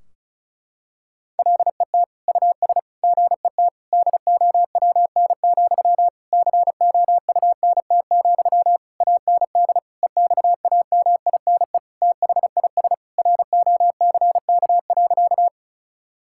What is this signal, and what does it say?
Morse code, heard as a keyed tone, about 36 wpm, 700 Hz. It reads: LET US GET DOWN, COUNT, AND EXAMINE THIS ROCK.
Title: Sentences from Top 1600 Words 35wpm